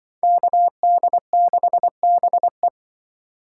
Morse code, heard as a keyed tone, about 25 wpm, 700 Hz. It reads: KD6BE
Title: Speed Racing - USA 2x2 Call Signs 20wpm